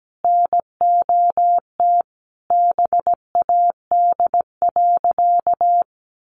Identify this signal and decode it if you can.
Morse code, heard as a keyed tone, about 17 wpm, 700 Hz. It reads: NOT BAD.
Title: Sentences from Top 900 Words 17wpm